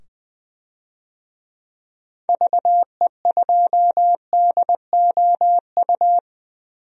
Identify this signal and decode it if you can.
Morse code, heard as a keyed tone, about 21 wpm, 700 Hz. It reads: VE2DOU
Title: DXCC-100 Call Signs - 6 Characters - 20wpm